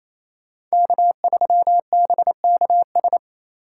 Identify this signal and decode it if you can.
Morse code, heard as a keyed tone, about 35 wpm, 700 Hz. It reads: K3BKS
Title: Speed Racing - USA 1x3 Call Signs 28wpm